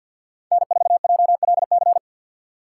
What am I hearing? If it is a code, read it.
Morse code, heard as a keyed tone, about 50 wpm, 700 Hz. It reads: N4JLC
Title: USA Call Signs - 5 Characters - 6x Character Spacing - 50wpm